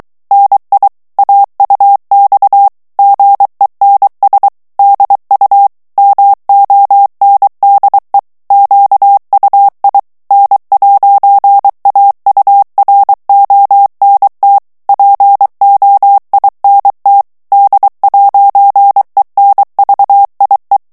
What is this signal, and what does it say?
Morse code, heard as a keyed tone, about 23 wpm, 800 Hz. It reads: NI AUX GENS DU MONDE QUI N'AURONT POINT D'ENVIE
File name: morse mp3 / FRANCAIS TEXTE CLAIR 2 VITESSE 22 MOTS MINUTES